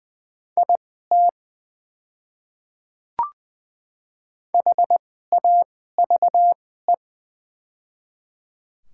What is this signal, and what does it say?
Morse code, heard as a keyed tone, about 20 wpm, spaced as for 14 wpm, 700 Hz. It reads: IT HAVE
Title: 2x Character Spacing - Top 100 Words - 20wpm